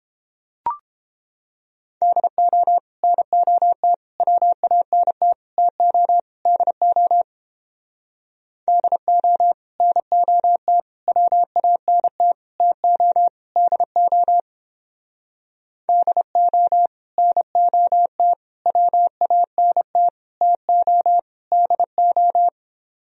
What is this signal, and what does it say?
Morse code, heard as a keyed tone, about 31 wpm, 700 Hz. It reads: DO NOT WANT TO DO DO NOT WANT TO DO DO NOT WANT TO DO